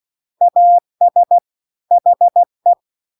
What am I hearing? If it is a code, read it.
Morse code, heard as a keyed tone, about 16 wpm, 700 Hz. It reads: AS HE
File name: CW-The-Big-Twoheaded-River-2-16WPM